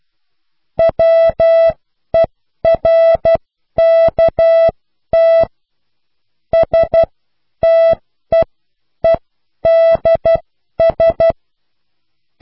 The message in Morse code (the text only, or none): WERKT STEEDS